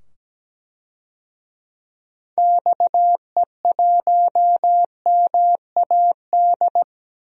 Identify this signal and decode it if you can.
Morse code, heard as a keyed tone, about 17 wpm, 700 Hz. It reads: XE1MAD
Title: DXCC-100 Call Signs - 6 Characters - 17wpm